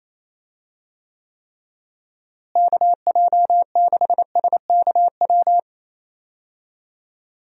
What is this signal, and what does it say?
Morse code, heard as a keyed tone, about 27 wpm, 700 Hz. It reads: KJ6SKW
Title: USA Call Signs - 6 Characters - Repeated 3x - 28wpm